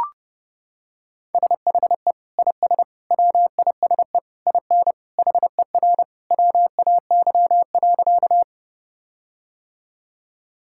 Morse code, heard as a keyed tone, about 30 wpm, 700 Hz. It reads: SHE IS WISE IN HER WAY.
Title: Sentences from Top 1600 Words 30wpm